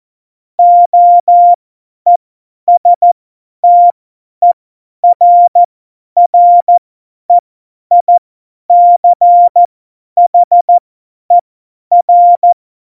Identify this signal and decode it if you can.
Morse code, heard as a keyed tone, about 14 wpm, spaced as for 10 wpm, 700 Hz. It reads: OESTERREICHER
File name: Morsekurs / Witze / Witz 24. November 2020, 50 F70